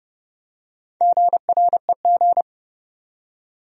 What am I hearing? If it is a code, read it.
Morse code, heard as a keyed tone, about 30 wpm, 700 Hz. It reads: GREG
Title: USA Common Names - Max 4 Letters - 30wpm